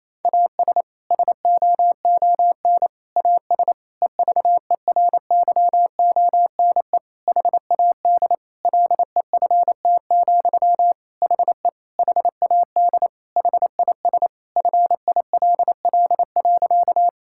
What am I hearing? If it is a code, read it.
Morse code, heard as a keyed tone, about 28 wpm, 700 Hz. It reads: AS SOON AS EVERYONE HAD LEFT, HE HAD HIS FILL.
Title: Sentences from Top 1000 Words - Review Part 5 - 28wpm